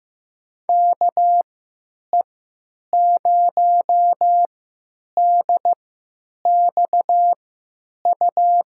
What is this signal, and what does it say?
Morse code, heard as a keyed tone, about 15 wpm, spaced as for 9 wpm, 700 Hz. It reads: KE0DXU
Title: USA Call Signs - 6 Characters - 3x Character Spacing - 15wpm